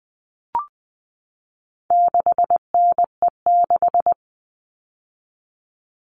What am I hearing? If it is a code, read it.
Morse code, heard as a keyed tone, about 20 wpm, 700 Hz. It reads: 6NE6
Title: All Cut Numbers - 4-Digit 20wpm